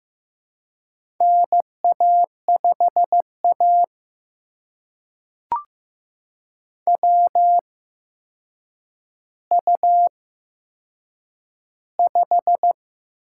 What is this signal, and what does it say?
Morse code, heard as a keyed tone, about 15 wpm, spaced as for 7 wpm, 700 Hz. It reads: NA5A WU5
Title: USA Call Signs - 4 Characters - 8x Character Spacing - 15wpm